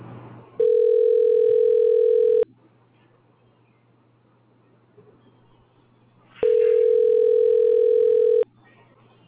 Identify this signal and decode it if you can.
Ringback tone (North America), 440 + 480 Hz.